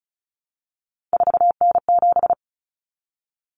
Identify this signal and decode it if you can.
Morse code, heard as a keyed tone, about 34 wpm, 700 Hz. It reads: 4N7
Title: Common Cut Numbers - 3-Digit 35wpm